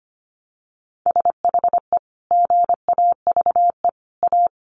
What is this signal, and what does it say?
Morse code, heard as a keyed tone, about 25 wpm, 700 Hz. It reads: SHE GAVE A
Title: Sentences from Top 1400 Words 25wpm